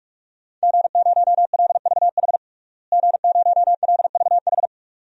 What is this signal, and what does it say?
Morse code, heard as a keyed tone, about 42 wpm, 700 Hz. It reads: G0LVH G0LVH
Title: DXCC-100 Call Signs - 5 Characters - Repeated 2x - 45wpm